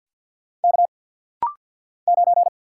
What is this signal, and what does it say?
Morse code, heard as a keyed tone, about 49 wpm, 700 Hz. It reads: K 9
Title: Single Letter-Number Alt - Rapid-Fire 50wpm